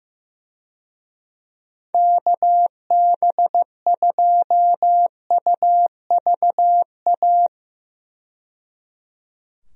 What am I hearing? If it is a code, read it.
Morse code, heard as a keyed tone, about 15 wpm, 700 Hz. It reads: KB2UVA
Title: USA Call Signs - 6 Characters - 15wpm